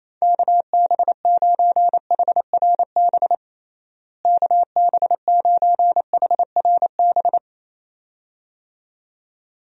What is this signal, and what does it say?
Morse code, heard as a keyed tone, about 28 wpm, 700 Hz. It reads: KB9HRB KB9HRB
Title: USA Call Signs - 6 Characters - Repeated 2x - 28wpm